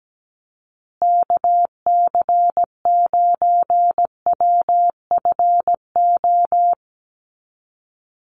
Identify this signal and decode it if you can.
Morse code, heard as a keyed tone, about 17 wpm, 700 Hz. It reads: KC9WFO
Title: USA Call Signs Repeated 3x 17wpm